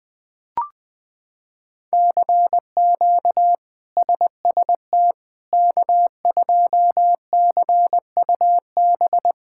Morse code, heard as a keyed tone, about 20 wpm, 700 Hz. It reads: CQ SST K2CUB
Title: K1USN SST QSO Practice - 20wpm